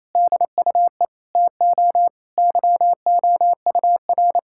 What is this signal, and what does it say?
Morse code, heard as a keyed tone, about 28 wpm, 690 Hz. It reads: DUE TO YOUR
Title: Sentences from Top 1000 Words - Review Part 6 - 28wpm